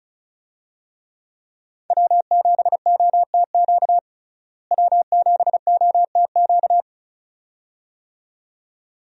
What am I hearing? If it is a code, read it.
Morse code, heard as a keyed tone, about 34 wpm, 700 Hz. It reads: W7OTQ W7OTQ
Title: USA Call Signs - 5 Characters - Repeated 2x - 35wpm